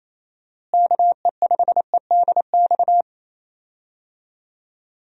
Morse code, heard as a keyed tone, about 28 wpm, 700 Hz. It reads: KE5EDX
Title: USA Call Signs - 6 Characters - Repeated 2x - 28wpm